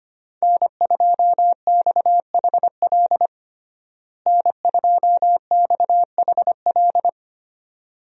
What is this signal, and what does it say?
Morse code, heard as a keyed tone, about 25 wpm, 700 Hz. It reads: N2XHL N2XHL
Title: USA Call Signs - 5 Characters - Repeated 2x - 25wpm